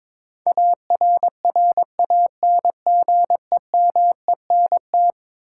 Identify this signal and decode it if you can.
Morse code, heard as a keyed tone, about 22 wpm, 700 Hz. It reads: ARRANGEMENT
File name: practice / 22wpm / Eleven Letter Words - 22wpm